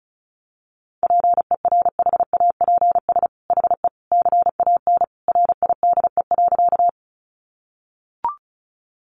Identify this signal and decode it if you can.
Morse code, heard as a keyed tone, about 34 wpm, 700 Hz. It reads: PERHAPS HE CAN RIDE.
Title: Sentences from Top 1200 Words 35wpm